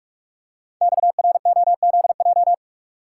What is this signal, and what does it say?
Morse code, heard as a keyed tone, about 44 wpm, 690 Hz. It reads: XROZJ
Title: Code Groups - 5 Letters - 45wpm